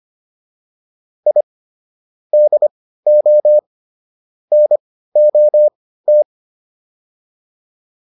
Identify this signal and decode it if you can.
Morse code, heard as a keyed tone, about 25 wpm, spaced as for 15 wpm, 600 Hz. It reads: I DO NOT